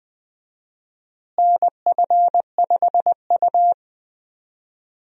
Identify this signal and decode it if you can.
Morse code, heard as a keyed tone, about 20 wpm, 700 Hz. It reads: NF5U
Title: USA Call Signs - 4 Characters - Repeated 2x - 20wpm